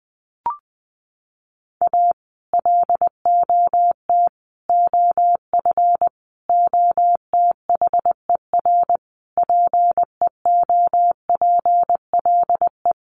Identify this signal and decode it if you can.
Morse code, heard as a keyed tone, about 20 wpm, 700 Hz. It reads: A LOT OF OTHER PEOPLE
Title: Sets of 5 Words Encore2 20wpm